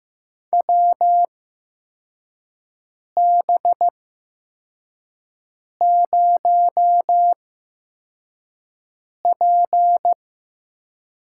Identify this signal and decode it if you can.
Morse code, heard as a keyed tone, about 15 wpm, spaced as for 4 wpm, 700 Hz. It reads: WB0P
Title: USA Call Signs - 6 Characters - 8x Character Spacing - 15wpm